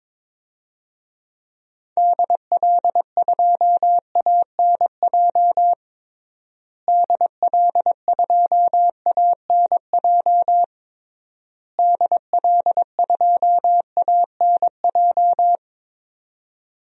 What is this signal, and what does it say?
Morse code, heard as a keyed tone, about 22 wpm, 700 Hz. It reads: DL2ANJ DL2ANJ DL2ANJ
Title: DXCC-100 Call Signs - 6 Characters - Repeated 3x - 22wpm